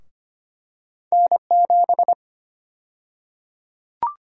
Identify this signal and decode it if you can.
Morse code, heard as a keyed tone, about 25 wpm, 700 Hz. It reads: N7
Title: USA Call Sign Prefixes - Repeated 2x - 25wpm